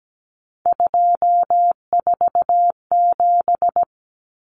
Morse code, heard as a keyed tone, about 17 wpm, 700 Hz. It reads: 247
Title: Numbers - 3-Digit 17wpm